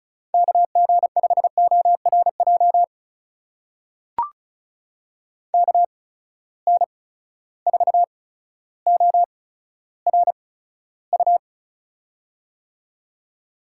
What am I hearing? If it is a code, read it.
Morse code, heard as a keyed tone, about 36 wpm, spaced as for 15 wpm, 700 Hz. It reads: KG5ORJ KN4ORU
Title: USA Call Signs - 6 Characters - 8x Character Spacing - 35wpm